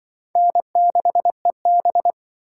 Morse code, heard as a keyed tone, about 24 wpm, 700 Hz. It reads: N6EB
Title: Speed Racing - USA 1x2 Call Signs 20wpm